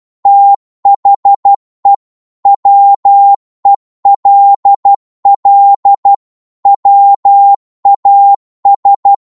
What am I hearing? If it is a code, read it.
Morse code, heard as a keyed tone, about 12 wpm, 800 Hz. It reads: THE WELL WAS
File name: cwmp3 / psg-1-12wpm